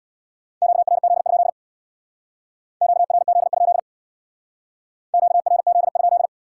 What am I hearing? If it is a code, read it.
Morse code, heard as a keyed tone, about 63 wpm, 700 Hz. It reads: QRZ? QRZ? QRZ?